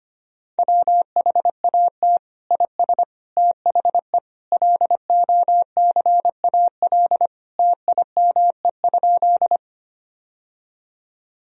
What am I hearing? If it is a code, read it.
Morse code, heard as a keyed tone, about 25 wpm, 700 Hz. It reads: WHAT IS THE LOCAL TIME?
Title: Sentences from Top 1000 Words - Review Part 3 - 25wpm